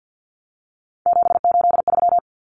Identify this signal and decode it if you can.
Morse code, heard as a keyed tone, about 50 wpm, 700 Hz. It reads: 783
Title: Numbers - 3-Digit 50wpm